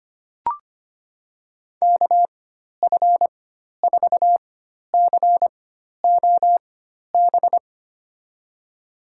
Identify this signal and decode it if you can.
Morse code, heard as a keyed tone, about 25 wpm, spaced as for 12 wpm, 700 Hz. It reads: KF4COB
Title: USA Call Signs - 6 Characters - 4x Character Spacing - 25wpm